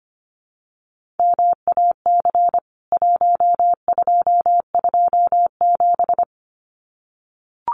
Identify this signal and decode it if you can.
Morse code, heard as a keyed tone, about 25 wpm, 700 Hz. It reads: MAC 1227
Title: CWT Exchange 25wpm